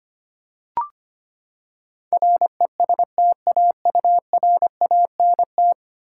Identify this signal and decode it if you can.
Morse code, heard as a keyed tone, about 25 wpm, 700 Hz. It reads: RESTAURANT
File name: Ten Letter Words - 25wpm